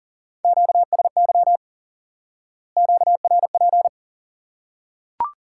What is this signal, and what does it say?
Morse code, heard as a keyed tone, about 40 wpm, 700 Hz. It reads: QSY QRP
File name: IQE Course 3 - Lesson 89 - YAGI - Two Elements - 40wpm